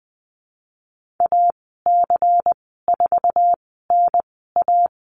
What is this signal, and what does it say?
Morse code, heard as a keyed tone, about 20 wpm, spaced as for 15 wpm, 700 Hz. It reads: AC4NA
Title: 128 Lesson 21g - Number 6 - Call Signs 20wpm